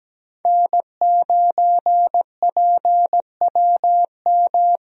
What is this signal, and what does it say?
Morse code, heard as a keyed tone, about 17 wpm, 700 Hz. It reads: N9PWM
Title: USA Call Signs - 5 Characters - 17wpm